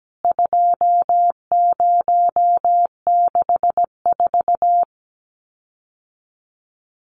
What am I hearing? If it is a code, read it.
Morse code, heard as a keyed tone, about 17 wpm, 700 Hz. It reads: 2064
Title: CWT Exchange 17wpm